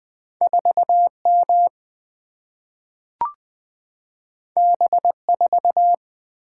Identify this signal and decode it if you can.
Morse code, heard as a keyed tone, about 20 wpm, 700 Hz. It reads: 4M B4
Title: Single-Letter Single-Number 20wpm